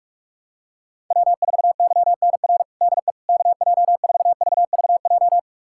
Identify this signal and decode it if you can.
Morse code, heard as a keyed tone, about 46 wpm, 700 Hz. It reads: W4YNR DE KJ4VVJ
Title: USA Callsign DE Callsign 45wpm